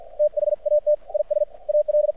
Morse code, heard as a keyed tone, about 53 wpm, spaced as for 47 wpm, 600 Hz. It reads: THAT IS AL
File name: archived / CRSnet